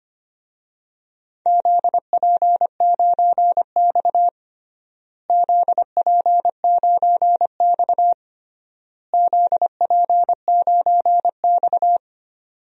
Morse code, heard as a keyed tone, about 25 wpm, 700 Hz. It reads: ZP9X ZP9X ZP9X
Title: DXCC-100 Call Signs - 4 Characters - Repeated 3x - 25wpm